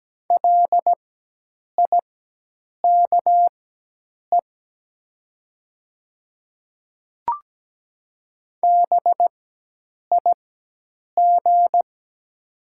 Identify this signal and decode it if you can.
Morse code, heard as a keyed tone, about 17 wpm, spaced as for 8 wpm, 700 Hz. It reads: LIKE BIG
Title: 4x Character Spacing - Top 500 Words - 17wpm